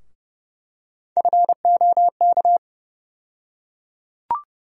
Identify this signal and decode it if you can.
Morse code, heard as a keyed tone, about 30 wpm, 700 Hz. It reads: FOK
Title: USA Call Sign Postfixes - 3 Letters 30wpm